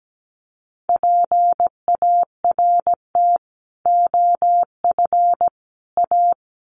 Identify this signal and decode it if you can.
Morse code, heard as a keyed tone, about 17 wpm, 700 Hz. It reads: PART OF A
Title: Sets of 3 Words 17wpm